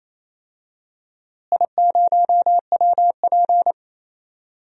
Morse code, heard as a keyed tone, about 28 wpm, 700 Hz. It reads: I0WP
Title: DXCC-100 Call Signs - 4 Characters - Repeated 3x - 28wpm